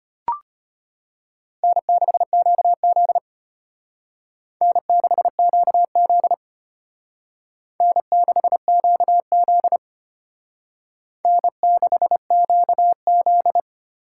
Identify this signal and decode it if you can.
Morse code, heard as a keyed tone, about 31 wpm, 700 Hz. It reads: N6QZ N6QZ N6QZ N6QZ
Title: Speed Racing - USA 1x2 Call Signs 25wpm